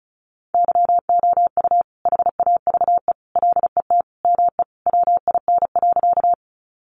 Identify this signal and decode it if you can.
Morse code, heard as a keyed tone, about 34 wpm, 700 Hz. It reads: YOU HAVE LET ME WIN.
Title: Sentences from Top 1500 Words 35wpm